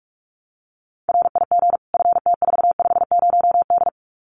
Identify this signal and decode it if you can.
Morse code, heard as a keyed tone, about 44 wpm, 700 Hz. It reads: RIG FT450D
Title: Common QSO Phrases 45wpm